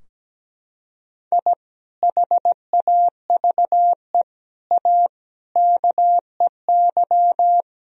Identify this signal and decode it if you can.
Morse code, heard as a keyed tone, about 17 wpm, 700 Hz. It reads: I HAVE A KEY
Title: Sentences from Top 1000 Words - Review Part 4 - 17wpm